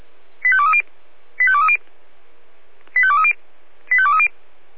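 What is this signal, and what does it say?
Five-tone selective call (ZVEI), heard as quick runs of five tones, tones of about 70 ms. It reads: 85230 85230 85230 85230
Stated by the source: Five-tone ZVEI selective call 85217, received on SDR.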